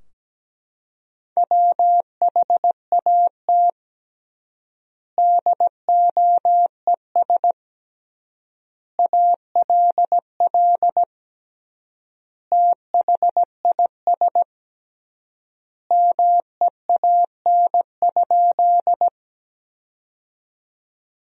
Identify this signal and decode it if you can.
Morse code, heard as a keyed tone, about 17 wpm, 700 Hz. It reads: WHAT DOES ALL THIS MEAN?
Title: 3x Word-Spacing - Sentences from Top 200 Words - 17wpm